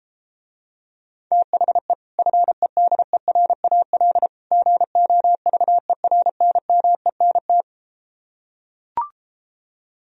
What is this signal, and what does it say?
Morse code, heard as a keyed tone, about 33 wpm, 700 Hz. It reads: THE FEDERAL GOVERNMENT